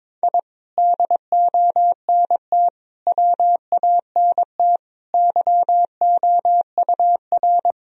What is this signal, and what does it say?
Morse code, heard as a keyed tone, about 22 wpm, 700 Hz. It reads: I DONT WANT YOUR
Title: Sentences from Top 1000 Words - Review Part 3 - 22wpm